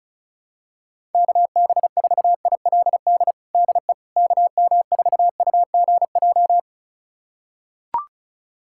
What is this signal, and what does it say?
Morse code, heard as a keyed tone, about 36 wpm, 700 Hz. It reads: KB4ILD DE KM4UGJ
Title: USA Callsign DE Callsign - 6 Characters - 35wpm